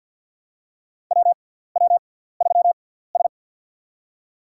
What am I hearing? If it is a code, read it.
Morse code, heard as a keyed tone, about 50 wpm, spaced as for 17 wpm, 700 Hz. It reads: WW3S
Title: USA Call Signs - 6x Character Spacing - 50wpm